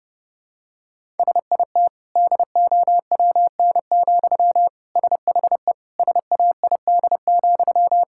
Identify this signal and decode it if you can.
Morse code, heard as a keyed tone, about 32 wpm, 700 Hz. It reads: SIT DOWN, SHE SAID,
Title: Sentences from Top 1000 Words - Review Part 6 - 30wpm